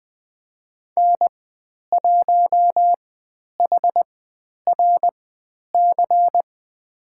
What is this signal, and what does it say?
Morse code, heard as a keyed tone, about 20 wpm, spaced as for 10 wpm, 700 Hz. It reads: N1HRC
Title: 095 Lesson 16g - Letter c - Call Signs 20wpm